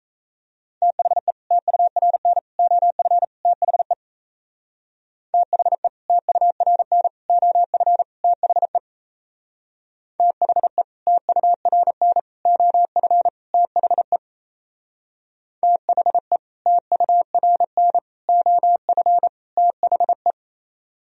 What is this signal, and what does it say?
Morse code, heard as a keyed tone, about 38 wpm, 700 Hz. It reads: THE TURN OF THE THE TURN OF THE THE TURN OF THE THE TURN OF THE